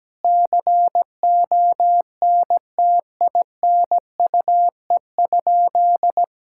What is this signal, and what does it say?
Morse code, heard as a keyed tone, about 17 wpm, 700 Hz. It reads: CONTINUE?
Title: Sentences from Top 1000 Words - Review Part 6 - 17wpm